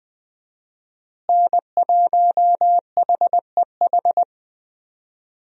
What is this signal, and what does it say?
Morse code, heard as a keyed tone, about 20 wpm, 700 Hz. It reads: N1HEH
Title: USA Call Signs - 5 Characters - Repeated 2x - 20wpm